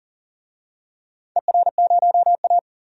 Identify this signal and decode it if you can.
Morse code, heard as a keyed tone, about 40 wpm, 700 Hz. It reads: ER0A
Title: DXCC-100 Call Signs - 4 Characters - 40wpm